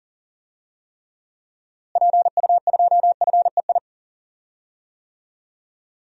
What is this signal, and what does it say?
Morse code, heard as a keyed tone, about 40 wpm, 700 Hz. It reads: PU2FEI
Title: DXCC-100 Call Signs - 6 Characters - Repeated 3x - 40wpm